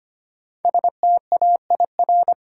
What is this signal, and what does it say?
Morse code, heard as a keyed tone, about 25 wpm, 700 Hz. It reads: STAIR
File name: 5 Letter Words - First 3 Letters Given - 25wpm